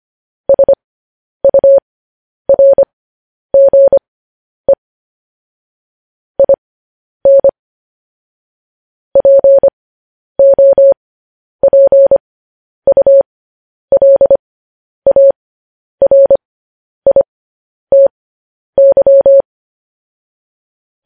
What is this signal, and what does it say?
Morse code, heard as a keyed tone, about 25 wpm, spaced as for 10 wpm, 550 Hz. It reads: SURGE IN POPULARITY
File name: podcast / News headlines-1x-25x10